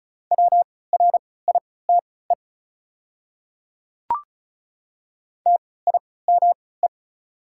Morse code, heard as a keyed tone, about 34 wpm, spaced as for 20 wpm, 700 Hz. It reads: WRITE TIME
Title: 3x Character Spacing - Top 100 Words - 35wpm